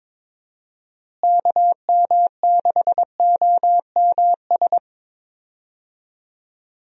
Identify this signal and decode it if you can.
Morse code, heard as a keyed tone, about 22 wpm, 700 Hz. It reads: KM6OMS
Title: USA Call Signs - 6 Characters - Repeated 2x - 22wpm